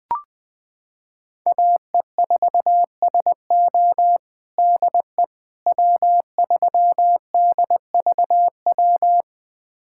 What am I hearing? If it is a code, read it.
Morse code, heard as a keyed tone, about 20 wpm, 700 Hz. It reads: AE4SO DE W3DVW
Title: USA Callsign DE Callsign - 5 Characters - 20wpm